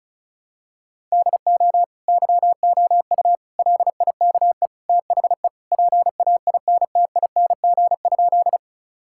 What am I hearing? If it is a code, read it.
Morse code, heard as a keyed tone, about 34 wpm, 700 Hz. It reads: DO YOU LIKE THE PAINTING?
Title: Sentences from Top 1000 Words - Review Part 6 - 35wpm